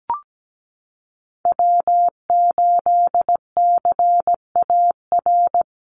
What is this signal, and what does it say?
Morse code, heard as a keyed tone, about 17 wpm, 700 Hz. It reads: W8CAR
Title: Call Signs - 17wpm